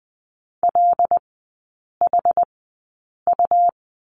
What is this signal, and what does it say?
Morse code, heard as a keyed tone, about 19 wpm, spaced as for 8 wpm, 700 Hz. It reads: LHU
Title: 100 Lesson 17e - Letter u - Three Characters 20wpm